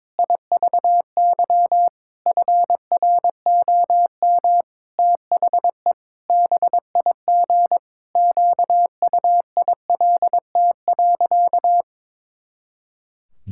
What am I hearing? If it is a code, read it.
Morse code, heard as a keyed tone, about 21 wpm, 700 Hz. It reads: IVY FROM THE BIG QUILT.
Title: Pangram Challenge v2 - 22wpm